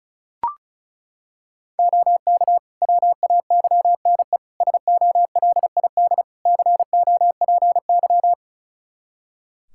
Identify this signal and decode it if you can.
Morse code, heard as a keyed tone, about 36 wpm, 700 Hz. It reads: OK WAYNE SOLID COPY
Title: Practice QSO - Exchange 3 - More Info - 35wpm